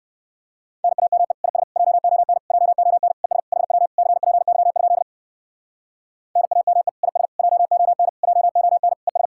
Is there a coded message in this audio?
Yes: Morse code from 0.8 to 9.3 s, heard as a keyed tone, about 59 wpm, 700 Hz. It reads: NAME IS JON JON ES HW CPY? NAME IS JON JON ES